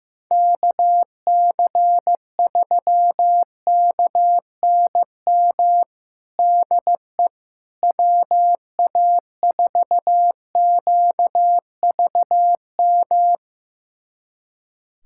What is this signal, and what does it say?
Morse code, heard as a keyed tone, about 15 wpm, 700 Hz. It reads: KC3KNM DE WA4QVM
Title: USA Callsign DE Callsign - 6 Characters - 15wpm